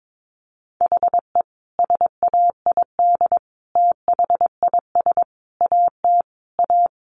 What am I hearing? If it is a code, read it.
Morse code, heard as a keyed tone, about 22 wpm, 700 Hz. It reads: HE SAID THIS AT A
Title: Sentences from Top 1000 Words 22wpm